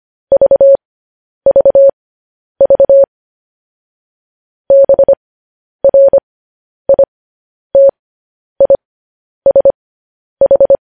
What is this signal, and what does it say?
Morse code, heard as a keyed tone, about 25 wpm, spaced as for 10 wpm, 550 Hz. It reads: VVV BRITISH
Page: News Headlines in Morse Code at 10 WPM